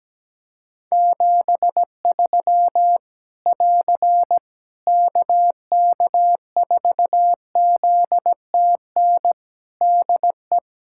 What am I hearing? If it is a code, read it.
Morse code, heard as a keyed tone, about 17 wpm, 700 Hz. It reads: 73 <AR> KK4ZTN DE
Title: K1USN SST QSO Practice 17wpm